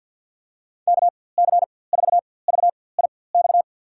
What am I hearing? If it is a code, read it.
Morse code, heard as a keyed tone, about 47 wpm, spaced as for 23 wpm, 700 Hz. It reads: KC4VIX
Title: USA Call Signs - 6 Characters - 4x Character Spacing - 50wpm